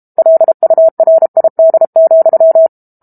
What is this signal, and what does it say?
Morse code, heard as a keyed tone, about 32 wpm, 650 Hz. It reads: LURID,
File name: Book Two / 06-Chapter 6 - 32 WPM